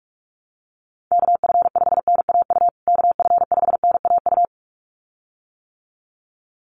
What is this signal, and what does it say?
Morse code, heard as a keyed tone, about 44 wpm, 700 Hz. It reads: KF5NAU KF5NAU
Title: USA Call Signs Repeated 2x 45wpm